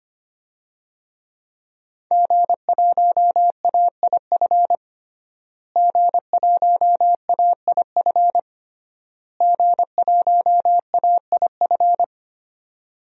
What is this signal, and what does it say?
Morse code, heard as a keyed tone, about 25 wpm, 700 Hz. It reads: G1AIF G1AIF G1AIF
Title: DXCC-100 Call Signs - 5 Characters - Repeated 3x - 25wpm